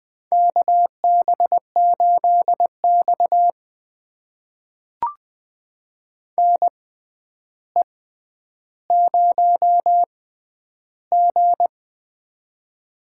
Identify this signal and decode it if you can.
Morse code, heard as a keyed tone, about 20 wpm, spaced as for 10 wpm, 700 Hz. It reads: KB8X NE0G
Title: USA Call Signs - 4 Characters - 6x Character Spacing - 20wpm